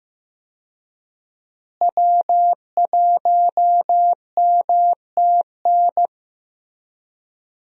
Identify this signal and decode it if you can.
Morse code, heard as a keyed tone, about 15 wpm, 700 Hz. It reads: W1MTN
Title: USA Call Signs - 5 Characters - Repeated 3x - 15wpm